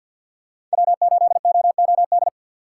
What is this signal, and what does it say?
Morse code, heard as a keyed tone, about 52 wpm, 700 Hz. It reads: W8OOD
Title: USA Call Signs - 5 Characters - 6x Character Spacing - 50wpm